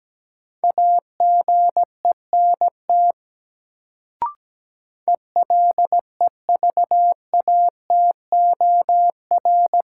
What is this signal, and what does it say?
Morse code, heard as a keyed tone, about 17 wpm, 700 Hz. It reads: AGENT ELEVATOR
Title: Sentences from Top 1600 Words - New Word Drill Alt - 17wpm